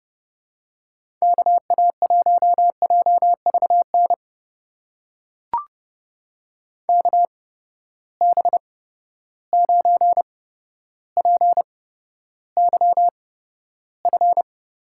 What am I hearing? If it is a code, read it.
Morse code, heard as a keyed tone, about 30 wpm, spaced as for 13 wpm, 700 Hz. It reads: KA1JVN KB9PYF
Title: USA Call Signs - 6 Characters - 8x Character Spacing - 30wpm